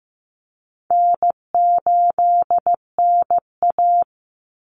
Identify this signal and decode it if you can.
Morse code, heard as a keyed tone, about 15 wpm, 700 Hz. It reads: N8NA
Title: Easy Call Signs - 15wpm